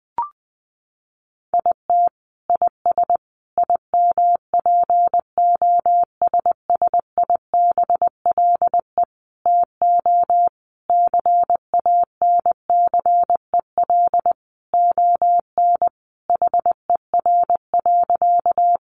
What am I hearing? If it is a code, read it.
Morse code, heard as a keyed tone, about 19 wpm, 700 Hz. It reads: IT IS IMPOSSIBLE TO CANCEL ON HER.
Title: Sentences from Top 1200 Words 20wpm